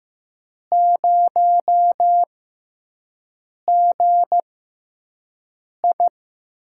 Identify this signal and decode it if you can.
Morse code, heard as a keyed tone, about 15 wpm, 700 Hz. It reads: 0 G I
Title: USA Call Signs - 6x Character Spacing - 15wpm